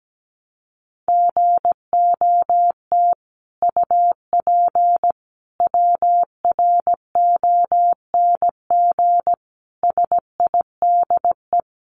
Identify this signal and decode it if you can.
Morse code, heard as a keyed tone, about 17 wpm, 700 Hz. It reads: GOT UP WRONG SIDE
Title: Sentences from Top 900 Words 17wpm